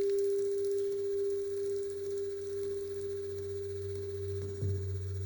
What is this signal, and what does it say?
Dial tone (Japan), 400 Hz.